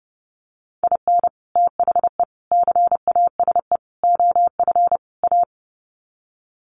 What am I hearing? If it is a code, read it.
Morse code, heard as a keyed tone, about 28 wpm, 700 Hz. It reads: IN THE CASE OF A
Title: Sets of 5 Words Encore 30wpm